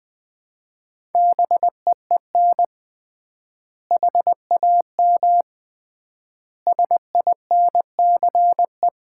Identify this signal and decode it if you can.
Morse code, heard as a keyed tone, about 20 wpm, 700 Hz. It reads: BEEN HAM SINCE
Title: Common QSO Phrases - 3x Word-Spacing - 20wpm